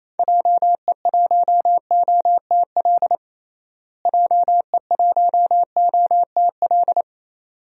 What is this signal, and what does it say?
Morse code, heard as a keyed tone, about 28 wpm, 700 Hz. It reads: JE1OTL JE1OTL
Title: DXCC-100 Call Signs - 6 Characters - Repeated 2x - 28wpm